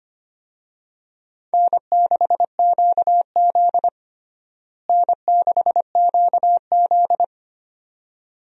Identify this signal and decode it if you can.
Morse code, heard as a keyed tone, about 25 wpm, 700 Hz. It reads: N6QZ N6QZ
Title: USA Call Signs - 4 Characters - Repeated 2x - 25wpm